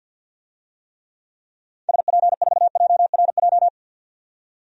Morse code, heard as a keyed tone, about 48 wpm, 700 Hz. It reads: SP4JRJ
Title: DXCC-100 Call Signs - 6 Characters - 50wpm